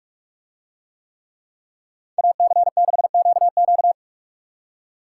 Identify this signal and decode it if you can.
Morse code, heard as a keyed tone, about 44 wpm, 700 Hz. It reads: AC6QQ